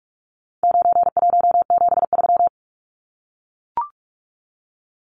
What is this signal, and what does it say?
Morse code, heard as a keyed tone, about 44 wpm, 700 Hz. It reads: 9173
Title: Numbers - 4-Digit 45wpm